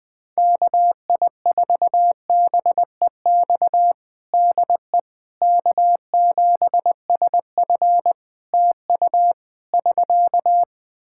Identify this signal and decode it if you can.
Morse code, heard as a keyed tone, about 19 wpm, 700 Hz. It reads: KI4BEX DE K7SF TU <SK>
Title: Practice QSO - Exchange 4 - Ending - 20wpm